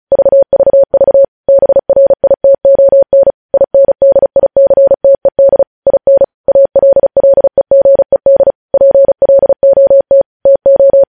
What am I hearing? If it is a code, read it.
Morse code, heard as a keyed tone, about 36 wpm, 550 Hz. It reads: VVV BRITON INDICTED IN ALLEGED PLOT TO
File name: podcast / News headlines-1x-35x35